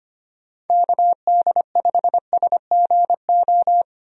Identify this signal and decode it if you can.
Morse code, heard as a keyed tone, about 25 wpm, 700 Hz. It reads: KD5SGO
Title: USA Call Signs - 6 Characters - 25wpm